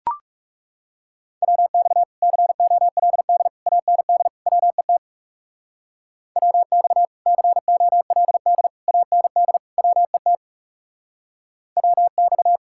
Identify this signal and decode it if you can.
Morse code, heard as a keyed tone, about 35 wpm, 700 Hz. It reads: WX COLD AND WET WX COLD AND WET WX
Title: Speed-Racing Common QSO Phrases - 30wpm